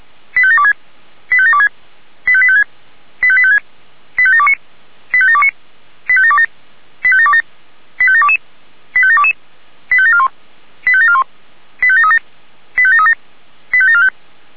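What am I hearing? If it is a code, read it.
Five-tone selective call (ZVEI), heard as quick runs of five tones, tones of about 70 ms. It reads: 85726 85726 85856 85856 85729 85729 85727 85727 85722 85722 85741 85741 85737 85737 85745
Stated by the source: ZVEI five-tone call received by SDR.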